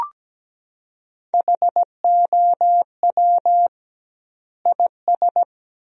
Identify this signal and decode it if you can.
Morse code, heard as a keyed tone, about 17 wpm, 700 Hz. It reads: HOW IS
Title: 2x Word-Spacing - Sentences from Top 100 Words - 17wpm